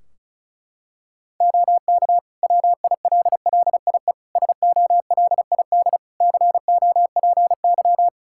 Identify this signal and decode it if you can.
Morse code, heard as a keyed tone, about 36 wpm, 700 Hz. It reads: OK WILLIE SOLID COPY
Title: Practice QSO - Exchange 3 - More Info - 35wpm